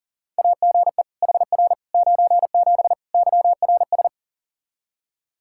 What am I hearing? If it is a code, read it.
Morse code, heard as a keyed tone, about 40 wpm, 700 Hz. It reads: AGE HR 97 YRS
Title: Practice QSO - Exchange 3 - More Info - 40wpm